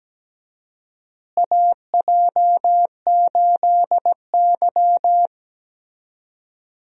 Morse code, heard as a keyed tone, about 17 wpm, 700 Hz. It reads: AJ8Y
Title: USA Call Signs - 4 Characters - Repeated 3x - 17wpm